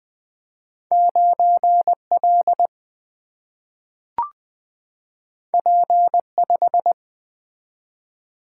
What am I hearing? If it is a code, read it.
Morse code, heard as a keyed tone, about 20 wpm, 700 Hz. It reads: 9L P5
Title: Two Letter-Number Combinations - 20wpm